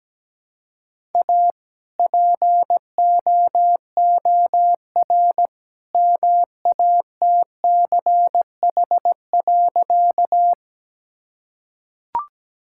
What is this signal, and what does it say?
Morse code, heard as a keyed tone, about 17 wpm, 700 Hz. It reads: A POOR MATCH.
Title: Sentences from Top 1600 Words 17wpm